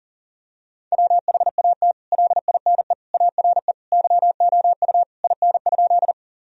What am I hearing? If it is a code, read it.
Morse code, heard as a keyed tone, about 40 wpm, 700 Hz. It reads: WHAT LINE ARE YOU IN?
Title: Sentences from Top 1000 Words - Review Part 1 - 40wpm